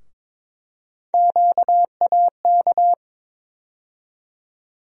Morse code, heard as a keyed tone, about 22 wpm, 700 Hz. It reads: QAK
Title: USA Call Sign Postfixes - 3 Letters 22wpm